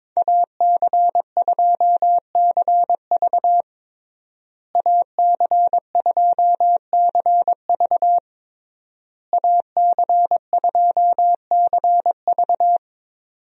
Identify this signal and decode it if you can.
Morse code, heard as a keyed tone, about 22 wpm, 700 Hz. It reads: AC2CV AC2CV AC2CV
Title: USA Call Signs - 5 Characters - Repeated 3x - 22wpm